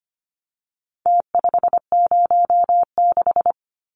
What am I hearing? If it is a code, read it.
Morse code, heard as a keyed tone, about 25 wpm, 700 Hz. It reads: T506
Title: Numbers - Contest Style - 4-Digit 25wpm